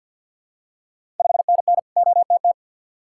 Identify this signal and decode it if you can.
Morse code, heard as a keyed tone, about 53 wpm, 700 Hz. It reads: 5NN OTT
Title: Ontario QSO Party Exchange 50wpm